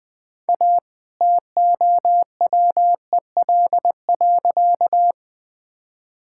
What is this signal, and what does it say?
Morse code, heard as a keyed tone, about 20 wpm, 700 Hz. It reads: A TOWEL.
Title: All-Age Appropriate Jokes - 20wpm